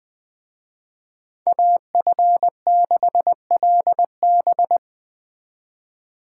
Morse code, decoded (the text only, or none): AF6LB